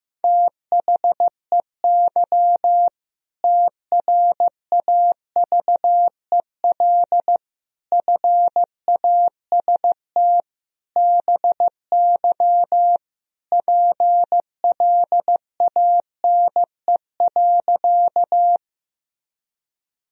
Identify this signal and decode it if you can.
Morse code, heard as a keyed tone, about 15 wpm, 700 Hz. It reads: THEY TRAVEL FAST BY PLANE.
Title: Sentences from Top 1000 Words - Review Part 3 - 15wpm